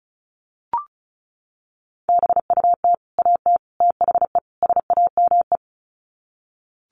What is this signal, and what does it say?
Morse code, heard as a keyed tone, about 34 wpm, 700 Hz. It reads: BUT AT THE SAME
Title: Sets of 4 Words 35wpm